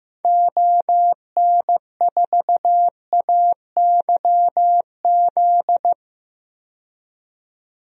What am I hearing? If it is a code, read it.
Morse code, heard as a keyed tone, about 15 wpm, 700 Hz. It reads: ON4AYZ
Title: DXCC-100 Call Signs - 6 Characters - 15wpm